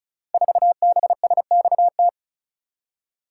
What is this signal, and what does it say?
Morse code, heard as a keyed tone, about 36 wpm, 700 Hz. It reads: 4BSXT
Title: Code Groups - 5 Letter:Numbers - 35wpm